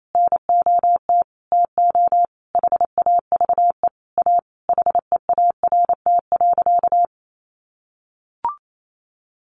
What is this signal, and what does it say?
Morse code, heard as a keyed tone, about 28 wpm, 700 Hz. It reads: NOT TO HAVE A HEART.
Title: Sentences from Top 1500 Words 28wpm